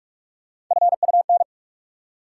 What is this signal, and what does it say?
Morse code, heard as a keyed tone, about 46 wpm, 700 Hz. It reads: FUN
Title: Three Letter Words - 45wpm